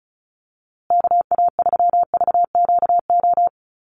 Morse code, heard as a keyed tone, about 34 wpm, 700 Hz. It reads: KA3VQO
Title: USA Call Signs Repeated 2x 35wpm